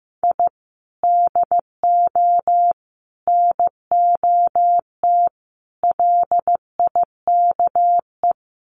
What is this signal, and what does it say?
Morse code, heard as a keyed tone, about 15 wpm, 700 Hz. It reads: I DO NOT LIKE
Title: Sets of 4 Words 15wpm